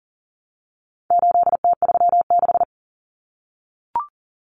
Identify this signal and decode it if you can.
Morse code, heard as a keyed tone, about 40 wpm, 700 Hz. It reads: 8T36
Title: Common Cut Numbers - 4-Digit 40wpm